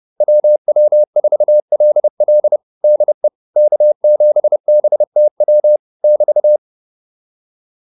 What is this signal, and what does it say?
Morse code, heard as a keyed tone, about 30 wpm, 600 Hz. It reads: WW4LL DE K7BTW <BT>